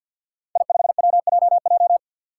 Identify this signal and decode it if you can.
Morse code, heard as a keyed tone, about 53 wpm, 700 Hz. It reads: I5WJJ
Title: Five Challenging Letter-Number Combinations 50wpm